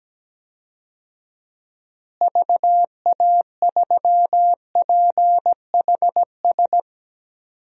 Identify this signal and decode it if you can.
Morse code, heard as a keyed tone, about 17 wpm, 700 Hz. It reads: VA3PHS
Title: DXCC-100 Call Signs - 6 Characters - Repeated 3x - 17wpm